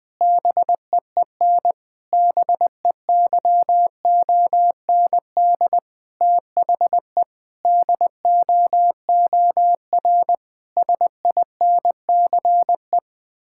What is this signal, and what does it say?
Morse code, heard as a keyed tone, about 20 wpm, 700 Hz. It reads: BEEN BEYOND THE DOOR SINCE
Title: Sentences from Top 1000 Words - Review Part 6 - 20wpm